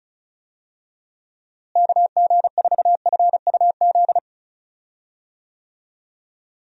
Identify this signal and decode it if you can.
Morse code, heard as a keyed tone, about 36 wpm, 700 Hz. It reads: KG4FUZ
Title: USA Call Signs - 6 Characters - Repeated 2x - 35wpm